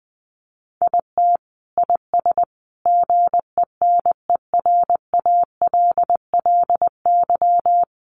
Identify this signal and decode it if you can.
Morse code, heard as a keyed tone, about 20 wpm, 700 Hz. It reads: IT IS GENERALLY
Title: Sentences from Top 1200 Words 20wpm